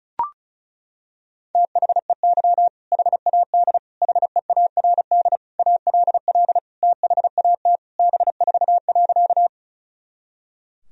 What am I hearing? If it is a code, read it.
Morse code, heard as a keyed tone, about 34 wpm, 700 Hz. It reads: THEY HAD HEARD ALL THAT B4.
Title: Sentences from Top 400 Words and Abbreviations 35wpm